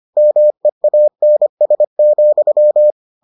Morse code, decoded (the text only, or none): MEANS,